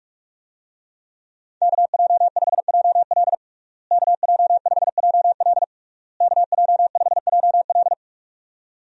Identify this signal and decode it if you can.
Morse code, heard as a keyed tone, about 47 wpm, 700 Hz. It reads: KJ5JL KJ5JL KJ5JL
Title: USA Call Signs - 5 Characters - Repeated 3x - 45wpm